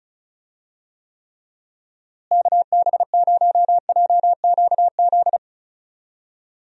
Morse code, heard as a keyed tone, about 36 wpm, 700 Hz. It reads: KB0JQZ
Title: USA Call Signs - 6 Characters - Repeated 3x - 35wpm